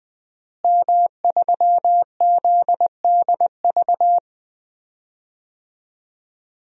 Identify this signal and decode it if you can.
Morse code, heard as a keyed tone, about 20 wpm, 700 Hz. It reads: M3ZDV
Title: DXCC-100 Call Signs - 5 Characters - Repeated 3x - 20wpm